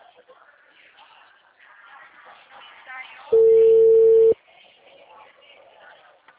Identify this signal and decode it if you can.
Ringback tone (China, 450 Hz), with speech.